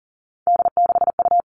Morse code, heard as a keyed tone, about 40 wpm, 700 Hz. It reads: D6U
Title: All Cut Numbers - 3-Digit 40wpm